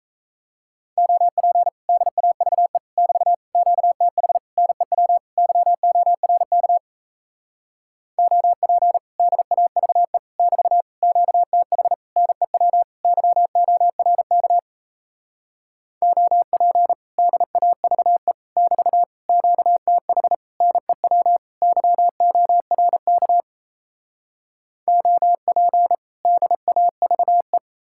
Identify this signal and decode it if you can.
Morse code, heard as a keyed tone, about 38 wpm, 700 Hz. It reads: OP DAVE <BT> QTH NEW YORK OP DAVE <BT> QTH NEW YORK OP DAVE <BT> QTH NEW YORK OP DAVE